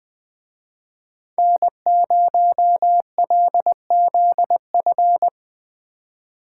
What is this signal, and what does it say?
Morse code, heard as a keyed tone, about 20 wpm, 700 Hz. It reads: N0LZF